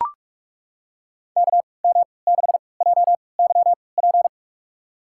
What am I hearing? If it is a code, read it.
Morse code, heard as a keyed tone, about 42 wpm, spaced as for 25 wpm, 700 Hz. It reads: KM6JYP
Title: USA Call Signs - 6 Characters - 3x Character Spacing - 45wpm